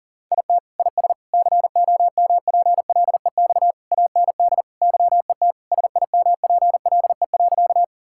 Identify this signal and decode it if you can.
Morse code, heard as a keyed tone, about 40 wpm, 700 Hz. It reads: IT IS COMPLEX AND YET SIMPLE.
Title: Sentences from Top 1000 Words - Review Part 6 - 40wpm